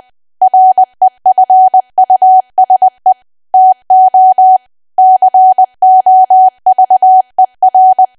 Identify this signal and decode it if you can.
Morse code, heard as a keyed tone, about 20 wpm, 750 Hz. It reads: REFUSE TO COVER
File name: cwa / W1AW-Level3 / 20WPM